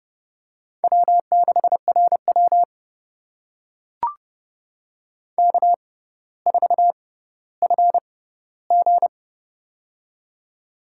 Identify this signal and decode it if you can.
Morse code, heard as a keyed tone, about 31 wpm, spaced as for 16 wpm, 690 Hz. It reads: W6RW K4FG
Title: USA Call Signs - 4 Characters - 6x Character Spacing - 30wpm